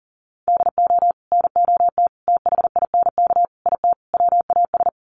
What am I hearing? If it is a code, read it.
Morse code, heard as a keyed tone, about 40 wpm, 700 Hz. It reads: DO NOT THINK IT WAS
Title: Sets of 5 Words Encore 40wpm